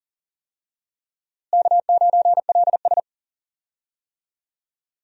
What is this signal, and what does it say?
Morse code, heard as a keyed tone, about 40 wpm, 700 Hz. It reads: K9LS
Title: USA Call Signs - 4 Characters - Repeated 2x - 40wpm